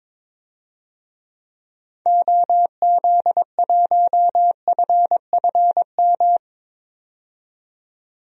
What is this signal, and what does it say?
Morse code, heard as a keyed tone, about 22 wpm, 700 Hz. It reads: OZ1FFM
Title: DXCC-100 Call Signs - 6 Characters - 22wpm